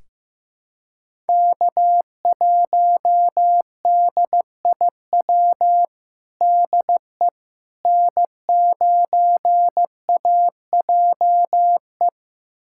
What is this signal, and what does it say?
Morse code, heard as a keyed tone, about 15 wpm, 700 Hz. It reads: K1DIW DE N9AJE
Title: USA Callsign DE Callsign - 5 Characters - 15wpm